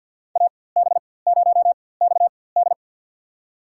Morse code, heard as a keyed tone, about 47 wpm, spaced as for 23 wpm, 700 Hz. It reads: AB0XD
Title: USA Call Signs - 5 Characters - 4x Character Spacing - 50wpm